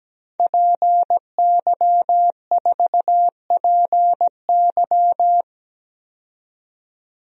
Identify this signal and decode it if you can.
Morse code, heard as a keyed tone, about 17 wpm, 700 Hz. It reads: PY4PY
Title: DXCC-100 Call Signs - 5 Characters - 17wpm